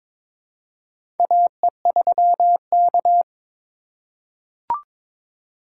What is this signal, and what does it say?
Morse code, heard as a keyed tone, about 21 wpm, 700 Hz. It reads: AE3K
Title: USA Call Signs - 4 Characters - 3x Character Spacing - 22wpm